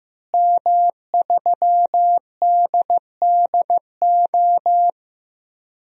Morse code, heard as a keyed tone, about 15 wpm, 700 Hz. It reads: M3DDO
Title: DXCC-100 Call Signs - 5 Characters - Repeated 3x - 15wpm